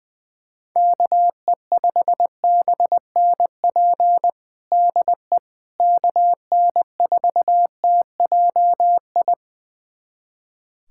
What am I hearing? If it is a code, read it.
Morse code, heard as a keyed tone, about 20 wpm, 700 Hz. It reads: KE5BNP DE KN4TJI
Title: USA Callsign DE Callsign 20wpm